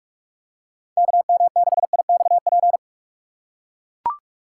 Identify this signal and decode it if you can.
Morse code, heard as a keyed tone, about 41 wpm, 700 Hz. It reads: KM6IXP
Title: USA Call Signs - 6 Characters - 6x Character Spacing - 45wpm